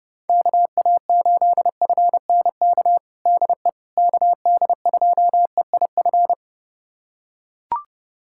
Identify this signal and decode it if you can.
Morse code, heard as a keyed tone, about 30 wpm, 690 Hz. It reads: KA8FNK DE KD2EIF
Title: USA Callsign DE Callsign - 6 Characters - 30wpm